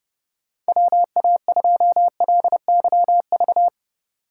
Morse code, heard as a keyed tone, about 30 wpm, 700 Hz. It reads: WA2LYV
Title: USA Call Signs - 6 Characters - 2x Character Spacing - 30wpm